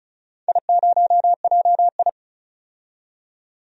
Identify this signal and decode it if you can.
Morse code, heard as a keyed tone, about 34 wpm, 690 Hz. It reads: I0JI
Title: DXCC-100 Call Signs - 4 Characters - 35wpm